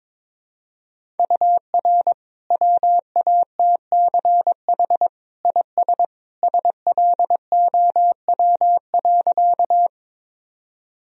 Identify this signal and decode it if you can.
Morse code, heard as a keyed tone, about 22 wpm, 700 Hz. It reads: UR WATCH IS SLOW.
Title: Sentences from Top 400 Words and Abbreviations 22wpm